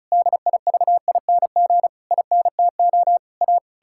Morse code, heard as a keyed tone, about 36 wpm, 700 Hz. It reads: DIVING INTO A
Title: Pangram Challenge v2 - 35wpm